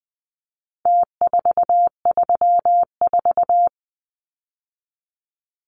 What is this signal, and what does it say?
Morse code, heard as a keyed tone, about 20 wpm, 700 Hz. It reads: T434
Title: Numbers - Contest Style - 4-Digit 20wpm